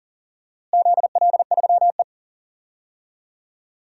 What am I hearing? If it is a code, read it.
Morse code, heard as a keyed tone, about 40 wpm, 700 Hz. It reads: ZL3E
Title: DXCC-100 Call Signs - 4 Characters - Repeated 3x - 40wpm